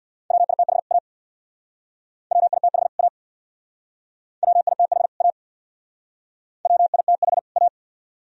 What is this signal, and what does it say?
Morse code, heard as a keyed tone, about 57 wpm, 700 Hz. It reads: WITH A WITH A WITH A WITH A
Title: Speed-Racing - Sets of 2 Words 50wpm